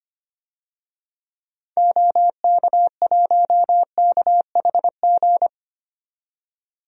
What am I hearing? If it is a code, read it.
Morse code, heard as a keyed tone, about 25 wpm, 700 Hz. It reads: OK1KHG
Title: DXCC-100 Call Signs - 6 Characters - 25wpm